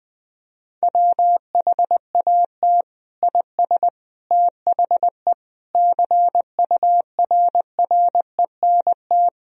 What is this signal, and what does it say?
Morse code, heard as a keyed tone, about 19 wpm, 700 Hz. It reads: WHAT IS THE CURRENT
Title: Sentences from Top 1000 Words - Review Part 4 - 20wpm